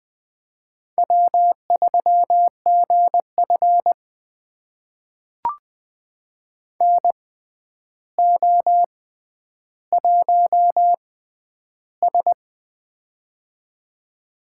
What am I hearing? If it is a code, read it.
Morse code, heard as a keyed tone, about 20 wpm, spaced as for 10 wpm, 700 Hz. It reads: W3GF NO1S
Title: USA Call Signs - 6x Character Spacing - 20wpm